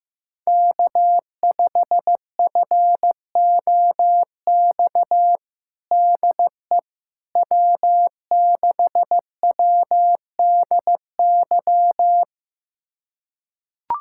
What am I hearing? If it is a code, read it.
Morse code, heard as a keyed tone, about 15 wpm, 700 Hz. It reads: K5FOX DE W6WDY
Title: USA Callsign DE Callsign - 5 Characters - 15wpm